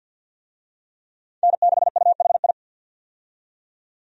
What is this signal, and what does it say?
Morse code, heard as a keyed tone, about 50 wpm, 700 Hz. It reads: N6UHI
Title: USA Call Signs - 5 Characters - 50wpm